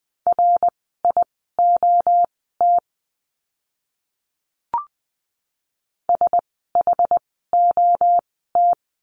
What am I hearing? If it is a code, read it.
Morse code, heard as a keyed tone, about 20 wpm, spaced as for 14 wpm, 700 Hz. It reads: RIOT SHOT
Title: 061 Lesson 11f - Letter h - Words 20wpm